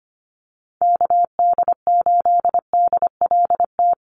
Morse code, heard as a keyed tone, about 25 wpm, 700 Hz. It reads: KD8DLT
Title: USA Call Signs 25wpm